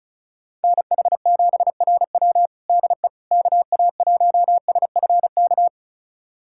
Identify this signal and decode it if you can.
Morse code, heard as a keyed tone, about 34 wpm, 700 Hz. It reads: NH7RW DE KA1SFK
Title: USA Callsign DE Callsign 35wpm